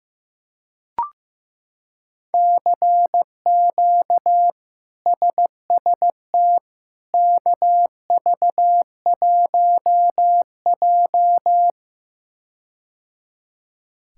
Morse code, heard as a keyed tone, about 15 wpm, 700 Hz. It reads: CQ SST KV1J
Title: K1USN SST QSO Practice - 15wpm